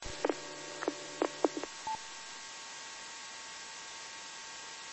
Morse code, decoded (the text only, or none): none